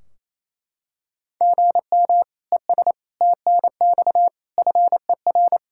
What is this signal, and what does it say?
Morse code, heard as a keyed tone, about 28 wpm, 700 Hz. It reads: GM ES TNX FER
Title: Practice QSO - Exchange 1 - Beginning - 28wpm